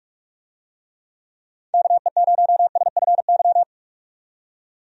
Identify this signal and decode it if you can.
Morse code, heard as a keyed tone, about 46 wpm, 700 Hz. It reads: KE0SFY
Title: USA Call Signs - 6 Characters - 45wpm